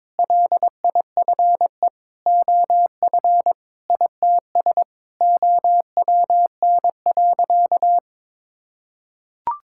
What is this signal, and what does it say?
Morse code, heard as a keyed tone, about 22 wpm, 700 Hz. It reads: LIFE OF ITS OWN.
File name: practice / 22wpm / Sentences with 8-Word Phrases - 22wpm